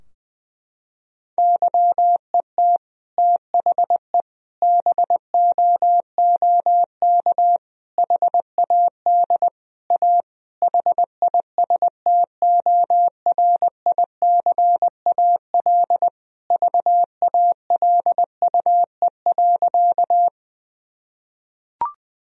Morse code, heard as a keyed tone, about 20 wpm, 700 Hz. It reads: YET THE BOOK HAD A HISTORICAL VALUE.
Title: Sentences from Top 1000 Words - Review Part 4 - 20wpm